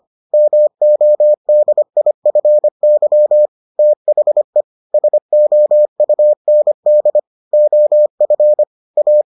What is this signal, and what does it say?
Morse code, heard as a keyed tone, about 25 wpm, 600 Hz. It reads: MODIFY THE SOUND OF A